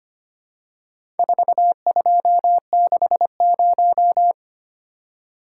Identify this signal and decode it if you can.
Morse code, heard as a keyed tone, about 25 wpm, 700 Hz. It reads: 4260
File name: practice / 25wpm / Four Letter-Number Combinations - 25wpm